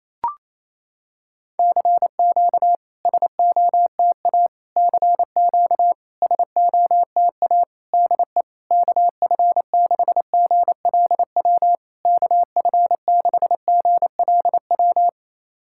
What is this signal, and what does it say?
Morse code, heard as a keyed tone, about 28 wpm, 700 Hz. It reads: CQ SOTA CQ SOTA DE KF6GLW KF6GLW
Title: SOTA QSO Practice 28wpm